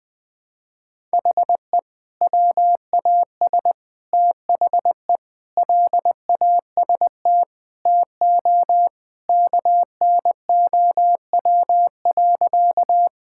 Morse code, heard as a keyed tone, about 20 wpm, 700 Hz. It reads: HE WAS THE LAST TO KNOW.
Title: Sentences with 2-Word Phrases - 20wpm